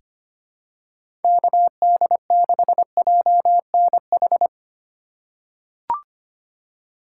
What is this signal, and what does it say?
Morse code, heard as a keyed tone, about 25 wpm, 700 Hz. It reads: KD6JNH
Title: USA Call Signs - 6 Characters - 3x Character Spacing - 25wpm